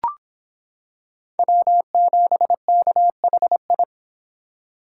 Morse code, heard as a keyed tone, about 25 wpm, 700 Hz. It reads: W7KHI